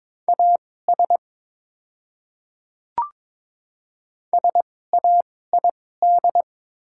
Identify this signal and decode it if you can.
Morse code, heard as a keyed tone, about 22 wpm, spaced as for 16 wpm, 700 Hz. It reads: AS SAID
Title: 2x Character Spacing - Top 100 Words - 22wpm